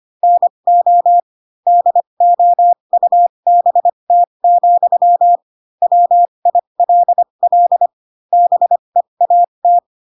Morse code, heard as a keyed tone, about 25 wpm, spaced as for 22 wpm, 700 Hz. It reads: NO DOUBT, WILL BEAT
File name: Pinocchio / Pinocchio ch3 18 25 ik1hge